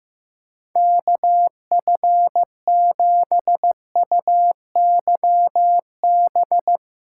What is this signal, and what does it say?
Morse code, heard as a keyed tone, about 16 wpm, 700 Hz. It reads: KF7UYB
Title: USA Callsign DE Callsign - 6 Characters - 15wpm